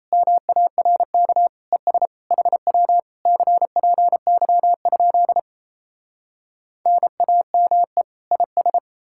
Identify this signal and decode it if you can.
Morse code, heard as a keyed tone, about 37 wpm, 700 Hz. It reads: MARK ES HW CPY? NAME IS